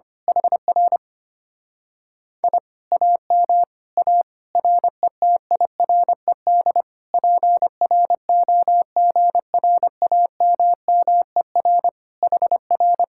Morse code, heard as a keyed tone, about 25 wpm, 700 Hz. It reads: HR I AM A RETIRED PROGRAMMER HR
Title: Speed-Racing Common QSO Phrases - 25wpm